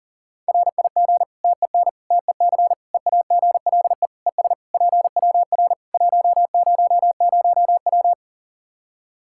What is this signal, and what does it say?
Morse code, heard as a keyed tone, about 40 wpm, 700 Hz. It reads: RIG TEN TEC EAGLE ES PWR 100W
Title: Practice QSO - Exchange 2 - Info - 40wpm